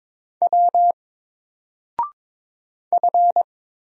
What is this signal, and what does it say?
Morse code, heard as a keyed tone, about 22 wpm, 700 Hz. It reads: W F
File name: Single-Letter-Number-22wpm